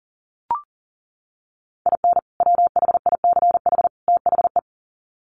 Morse code, heard as a keyed tone, about 40 wpm, 700 Hz. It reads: IN WHICH THE
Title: Sets of 3 Words 40wpm